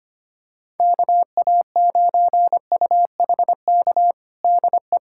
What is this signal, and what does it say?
Morse code, heard as a keyed tone, about 25 wpm, 700 Hz. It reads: KA9UHK DE
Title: USA Callsign DE Callsign - 6 Characters - 25wpm